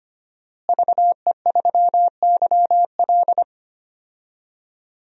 Morse code, heard as a keyed tone, about 25 wpm, 700 Hz. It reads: VE3YL
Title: DXCC-100 Call Signs - 5 Characters - Repeated 3x - 25wpm